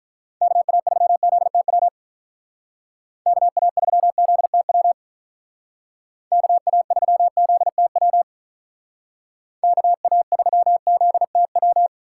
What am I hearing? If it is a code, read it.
Morse code, heard as a keyed tone, about 43 wpm, 700 Hz. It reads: KA3ZTW KA3ZTW KA3ZTW KA3ZTW
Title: Speed Racing - USA 2x3 Call Signs 35wpm